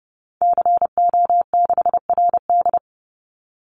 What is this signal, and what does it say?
Morse code, heard as a keyed tone, about 30 wpm, 700 Hz. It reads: CO6RD
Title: Call Signs - 30wpm